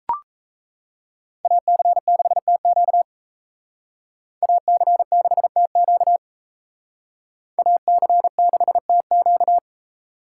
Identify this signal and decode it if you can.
Morse code, heard as a keyed tone, about 36 wpm, 700 Hz. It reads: AC6TQ AC6TQ AC6TQ